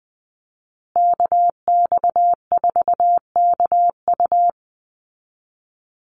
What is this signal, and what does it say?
Morse code, heard as a keyed tone, about 20 wpm, 700 Hz. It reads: KX4KU
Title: Call Signs Repeated 2x - 20wpm